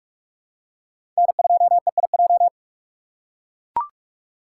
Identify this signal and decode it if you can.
Morse code, heard as a keyed tone, about 44 wpm, 700 Hz. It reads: N2EIJ
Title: USA Call Signs - 5 Characters - 3x Character Spacing - 45wpm